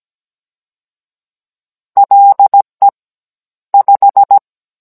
Morse code, heard as a keyed tone, about 17 wpm, 800 Hz. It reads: LE 5